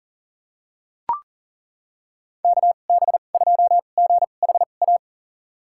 Morse code, heard as a keyed tone, about 40 wpm, spaced as for 29 wpm, 700 Hz. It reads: KB2GHA
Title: USA Call Signs - 6 Characters - 2x Character Spacing - 40wpm